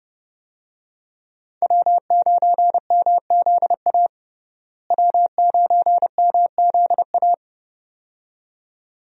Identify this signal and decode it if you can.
Morse code, heard as a keyed tone, about 31 wpm, 700 Hz. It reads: W9MZA W9MZA
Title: USA Call Signs - 5 Characters - Repeated 2x - 30wpm